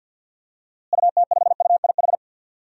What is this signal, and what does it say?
Morse code, heard as a keyed tone, about 47 wpm, 700 Hz. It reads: UT5UIH
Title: DXCC-100 Call Signs - 6 Characters - Repeated 2x - 50wpm